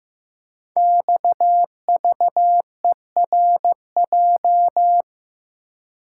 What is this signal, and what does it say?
Morse code, heard as a keyed tone, about 15 wpm, 700 Hz. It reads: XVERJ
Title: Code Groups - 5 Letters - 15wpm